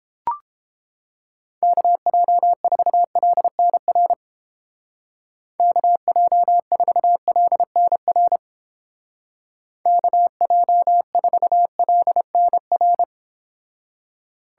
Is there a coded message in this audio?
Yes: Morse code from 1.6 to 13.0 s, heard as a keyed tone, about 29 wpm, 700 Hz. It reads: KJ4LNR KJ4LNR KJ4LNR